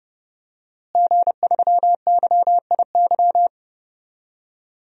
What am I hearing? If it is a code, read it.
Morse code, heard as a keyed tone, about 30 wpm, 700 Hz. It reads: G3YIY
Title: DXCC-100 Call Signs - 5 Characters - Repeated 2x - 30wpm